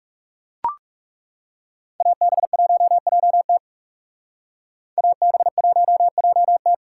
Morse code, heard as a keyed tone, about 36 wpm, 700 Hz. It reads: AB1JT AB1JT
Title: Speed Racing - USA 2x2 Call Signs 30wpm